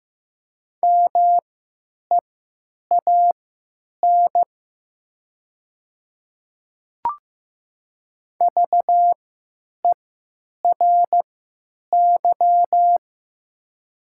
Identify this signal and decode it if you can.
Morse code, heard as a keyed tone, about 15 wpm, spaced as for 9 wpm, 700 Hz. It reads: MEAN VERY
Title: 3x Character Spacing - Top 200 Words - 15wpm